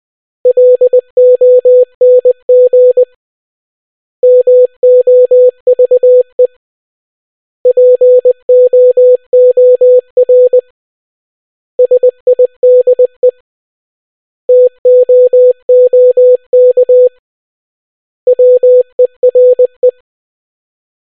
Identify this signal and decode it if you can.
Morse code, heard as a keyed tone, about 20 wpm, 500 Hz. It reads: LONG MOVE POOR SIDE TOOK WERE